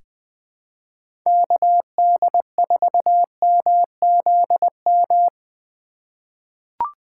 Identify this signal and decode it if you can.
Morse code, heard as a keyed tone, about 20 wpm, 700 Hz. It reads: KD4MZM
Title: 2023 POTA Activator Call Signs - 20wpm